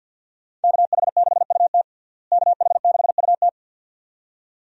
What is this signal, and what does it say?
Morse code, heard as a keyed tone, about 48 wpm, 700 Hz. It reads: KH6UT KH6UT
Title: USA Call Signs - 5 Characters - Repeated 2x - 50wpm